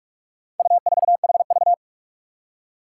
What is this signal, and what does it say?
Morse code, heard as a keyed tone, about 41 wpm, 700 Hz. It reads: U4HV
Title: Four Challenging Letter-Number Combinations 45wpm